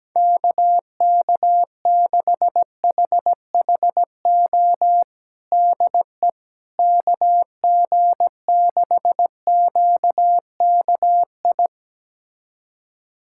Morse code, heard as a keyed tone, about 17 wpm, 700 Hz. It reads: KK6HHO DE KG6QKI
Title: USA Callsign DE Callsign 17wpm